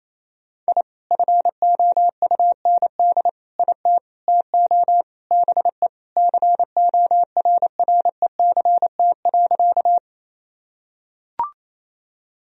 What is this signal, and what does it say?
Morse code, heard as a keyed tone, about 28 wpm, 700 Hz. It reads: I FOUND IT TO BE CORRECT.
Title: Sentences from Top 1000 Words - Review Part 2 - 28wpm